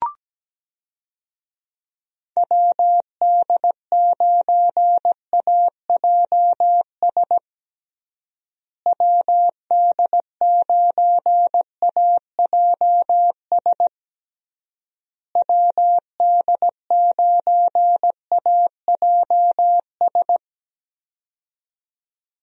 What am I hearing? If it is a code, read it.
Morse code, heard as a keyed tone, about 17 wpm, 700 Hz. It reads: WD9AJS WD9AJS WD9AJS
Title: USA Call Signs - 6 Characters - Repeated 3x - 17wpm